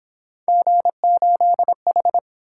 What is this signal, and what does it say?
Morse code, heard as a keyed tone, about 26 wpm, 700 Hz. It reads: G8H
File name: Speed-Racing - Three Letter-Number Combinations - 22wpm